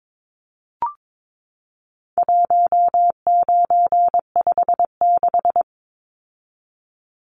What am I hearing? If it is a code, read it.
Morse code, heard as a keyed tone, about 22 wpm, 700 Hz. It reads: 1956
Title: Numbers - 4-Digit 22wpm